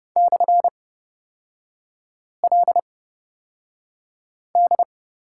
Morse code, heard as a keyed tone, about 30 wpm, spaced as for 5 wpm, 700 Hz. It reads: /LD
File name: Lesson 36e - BT - Three Characters 30wpm